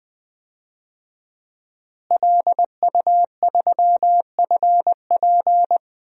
Morse code, heard as a keyed tone, about 20 wpm, 700 Hz. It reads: LU3FP
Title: DXCC-100 Call Signs - 5 Characters - Repeated 2x - 20wpm